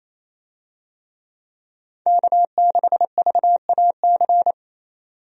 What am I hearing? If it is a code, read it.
Morse code, heard as a keyed tone, about 28 wpm, 700 Hz. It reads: K6VAC
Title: USA Call Signs - 5 Characters - 28wpm